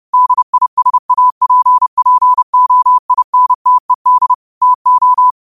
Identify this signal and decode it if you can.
Morse code, heard as a keyed tone, about 30 wpm, 1000 Hz. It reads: DISAPPOINTED TO